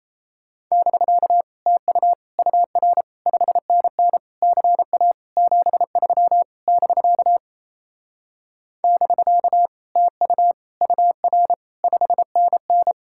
Morse code, heard as a keyed tone, about 38 wpm, 700 Hz. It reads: <BK> TU UR 5NN CA 73 <BK> <BK> TU UR 5NN